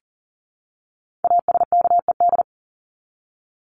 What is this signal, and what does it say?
Morse code, heard as a keyed tone, about 40 wpm, 700 Hz. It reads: ASKED
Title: ED Suffix - Top 100 Words with Base Words 40wpm